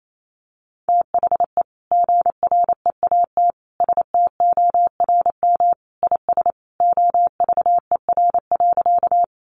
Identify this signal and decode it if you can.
Morse code, heard as a keyed tone, about 28 wpm, 700 Hz. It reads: THE GREAT STORM IS OVER.
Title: Sentences from Top 1500 Words 28wpm